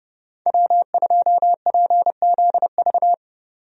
Morse code, heard as a keyed tone, about 32 wpm, 700 Hz. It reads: W2PZV
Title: USA Call Signs - 5 Characters - 30wpm